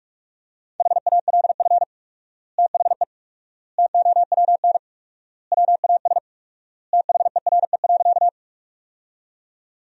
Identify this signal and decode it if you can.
Morse code, heard as a keyed tone, about 47 wpm, 690 Hz. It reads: HALF THE TOWN WAS THERE.
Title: 4x Word-Spacing - Sentences from Top 500 Words - 45wpm